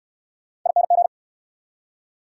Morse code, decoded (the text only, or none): ING